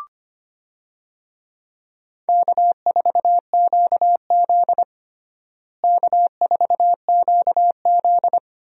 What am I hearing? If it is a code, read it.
Morse code, heard as a keyed tone, about 25 wpm, 700 Hz. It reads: K4QZ K4QZ
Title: USA Call Signs - 4 Characters - Repeated 2x - 25wpm